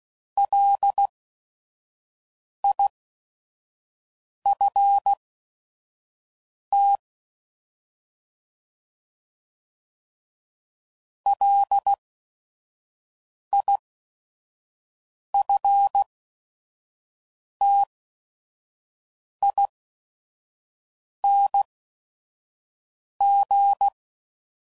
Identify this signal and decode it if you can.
Morse code, heard as a keyed tone, about 16 wpm, spaced as for 5 wpm, 800 Hz. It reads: LIFT LIFTING